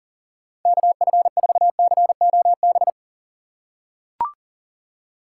Morse code, heard as a keyed tone, about 40 wpm, 700 Hz. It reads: KF4COB
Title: USA Call Signs - 6 Characters - 3x Character Spacing - 40wpm